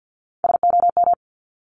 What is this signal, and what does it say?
Morse code, heard as a keyed tone, about 50 wpm, 700 Hz. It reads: SOM
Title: Top Three Letter Combinations in Words 50wpm